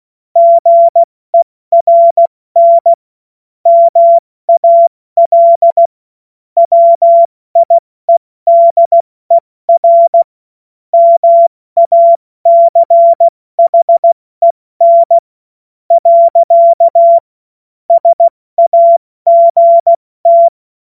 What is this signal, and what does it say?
Morse code, heard as a keyed tone, about 16 wpm, spaced as for 14 wpm, 680 Hz. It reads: GERN MAL WIEDER MACHEN. SAGT
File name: Morsekurs / Witze / Witz 13. April 2021, 70 F80